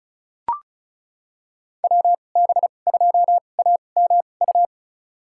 Morse code, heard as a keyed tone, about 34 wpm, spaced as for 25 wpm, 690 Hz. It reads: WB2AMU
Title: USA Call Signs - 6 Characters - 2x Character Spacing - 35wpm